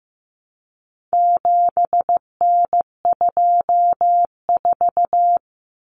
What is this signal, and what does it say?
Morse code, heard as a keyed tone, about 15 wpm, 700 Hz. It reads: 7N24
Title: Common Cut Numbers - 4-Digit 15wpm